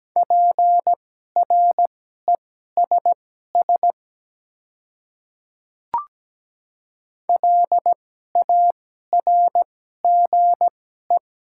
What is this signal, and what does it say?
Morse code, heard as a keyed tone, about 17 wpm, spaced as for 12 wpm, 700 Hz. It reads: PRESS LARGE
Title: 2x Character Spacing - Top 300 Words - 17wpm